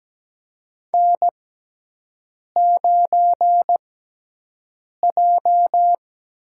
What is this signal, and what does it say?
Morse code, heard as a keyed tone, about 17 wpm, spaced as for 6 wpm, 700 Hz. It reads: N9J
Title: USA Call Signs - 6x Character Spacing - 17wpm